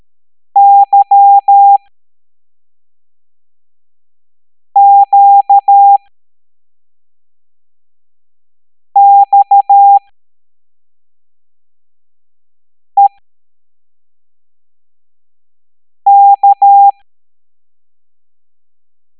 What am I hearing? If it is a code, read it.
Morse code, heard as a keyed tone, about 13 wpm, spaced as for 3 wpm, 800 Hz. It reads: YQXEK
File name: hamdata / L09-EX04